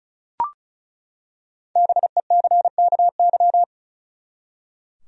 Morse code, heard as a keyed tone, about 36 wpm, 700 Hz. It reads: BECKY
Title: USA Common Names - 35wpm